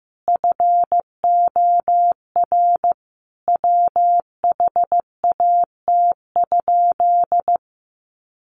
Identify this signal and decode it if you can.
Morse code, heard as a keyed tone, about 15 wpm, 700 Hz. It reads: FOR WHAT?
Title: Sentences from Top 100 Words 15wpm